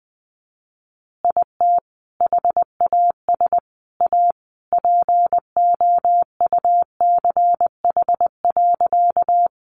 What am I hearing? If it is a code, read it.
Morse code, heard as a keyed tone, about 20 wpm, 700 Hz. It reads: IT HAS A POUCH.
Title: Easy Phrases 20wpm